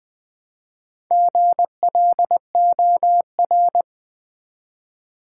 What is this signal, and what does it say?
Morse code, heard as a keyed tone, about 20 wpm, 700 Hz. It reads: GLOR
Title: Build A Word - 1 Letter At A Time - 5-Letter Words - 20wpm